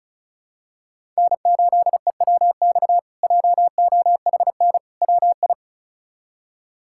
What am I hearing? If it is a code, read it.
Morse code, heard as a keyed tone, about 36 wpm, 700 Hz. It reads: N8EWX JOHN WI
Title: K1USN SST QSO Practice - 35wpm